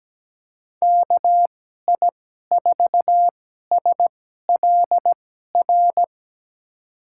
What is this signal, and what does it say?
Morse code, heard as a keyed tone, about 17 wpm, spaced as for 12 wpm, 700 Hz. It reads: KI4SLR
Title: USA Call Signs - 6 Characters - 2x Character Spacing - 17wpm